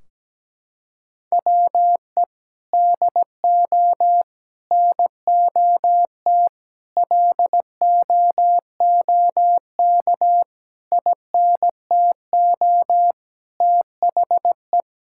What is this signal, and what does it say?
Morse code, heard as a keyed tone, about 17 wpm, 700 Hz. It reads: WE DO NOT LOOK INTO THE
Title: Sentences from Top 1600 Words 17wpm